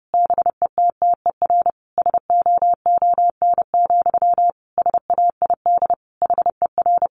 Morse code, heard as a keyed tone, about 30 wpm, 700 Hz. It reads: BETTER SOON, SAID HER
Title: Sentences from Top 1500 Words 30wpm